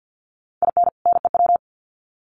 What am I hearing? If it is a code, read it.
Morse code, heard as a keyed tone, about 50 wpm, 700 Hz. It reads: IN NEW
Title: Sets of 2 Words 50wpm